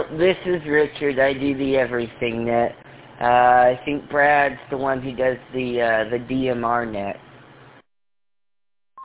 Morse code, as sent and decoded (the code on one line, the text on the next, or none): none